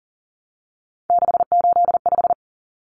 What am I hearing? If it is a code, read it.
Morse code, heard as a keyed tone, about 37 wpm, 700 Hz. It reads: 685
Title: Numbers - 3-Digit 40wpm